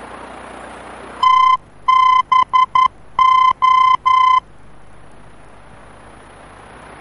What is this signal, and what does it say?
Morse code, heard as a keyed tone, about 11 wpm, 1030 Hz. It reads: TBO
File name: france / mx / TBO-F-VOR